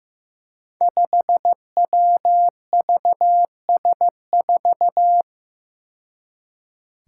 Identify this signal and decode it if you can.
Morse code, heard as a keyed tone, about 15 wpm, 700 Hz. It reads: 5WVS4
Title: Five Challenging Letter-Number Combinations 15wpm